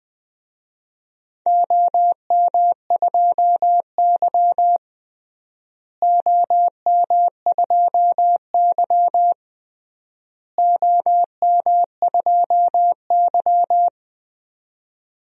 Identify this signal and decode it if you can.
Morse code, heard as a keyed tone, about 20 wpm, 700 Hz. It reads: OM2Y OM2Y OM2Y
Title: DXCC-100 Call Signs - 4 Characters - Repeated 3x - 20wpm